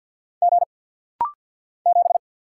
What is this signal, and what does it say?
Morse code, heard as a keyed tone, about 50 wpm, 690 Hz. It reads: G 7
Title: Single Letter-Number Alt - Rapid-Fire 50wpm